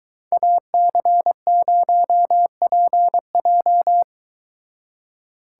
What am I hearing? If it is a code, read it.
Morse code, heard as a keyed tone, about 23 wpm, 700 Hz. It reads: AC0PJ